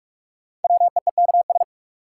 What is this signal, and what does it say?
Morse code, heard as a keyed tone, about 43 wpm, 700 Hz. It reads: WEEKS
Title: Sets of 5 Words - New Word Drill - 45wpm